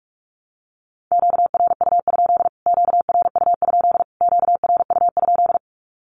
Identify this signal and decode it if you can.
Morse code, heard as a keyed tone, about 44 wpm, 700 Hz. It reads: QRU? QRU? QRU?
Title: Q-Codes Repeated 3x 45wpm